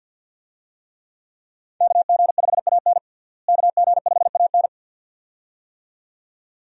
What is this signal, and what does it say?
Morse code, heard as a keyed tone, about 51 wpm, 700 Hz. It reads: KG5AN KG5AN
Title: USA Call Signs - 5 Characters - Repeated 2x - 50wpm